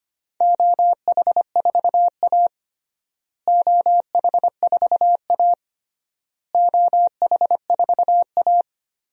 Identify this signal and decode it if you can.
Morse code, heard as a keyed tone, about 25 wpm, 700 Hz. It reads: OH4A OH4A OH4A
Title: DXCC-100 Call Signs - 4 Characters - Repeated 3x - 25wpm